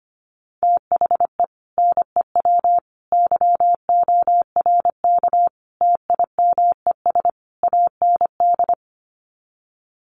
Morse code, heard as a keyed tone, about 25 wpm, 700 Hz. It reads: THE NEW YORK TIMES AND
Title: Sets of 5 Words Encore 25wpm